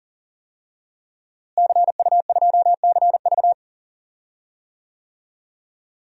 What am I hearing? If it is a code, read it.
Morse code, heard as a keyed tone, about 40 wpm, 700 Hz. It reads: CU2CV
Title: DXCC-100 Call Signs - 5 Characters - Repeated 3x - 40wpm